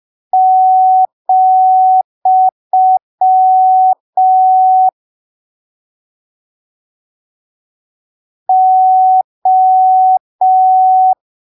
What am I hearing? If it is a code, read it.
Morse code, heard as a keyed tone, about 5 wpm, 750 Hz. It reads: , O
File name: arquivos / texto / texto 23 05ppm